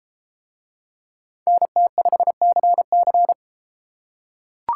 Morse code, heard as a keyed tone, about 31 wpm, 700 Hz. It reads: NT5CC